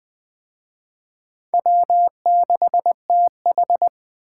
Morse code, heard as a keyed tone, about 20 wpm, 700 Hz. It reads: W6TH